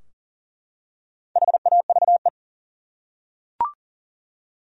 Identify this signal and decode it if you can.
Morse code, heard as a keyed tone, about 40 wpm, 700 Hz. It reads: HAVE
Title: Top 100 Words 40wpm